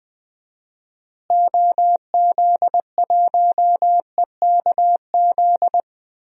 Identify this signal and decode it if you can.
Morse code, heard as a keyed tone, about 20 wpm, 700 Hz. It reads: OZ1EKZ
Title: DXCC-100 Call Signs - 6 Characters - Repeated 3x - 20wpm